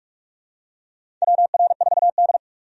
Morse code, heard as a keyed tone, about 46 wpm, 700 Hz. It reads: WR4D
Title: USA Call Signs - 4 Characters - 45wpm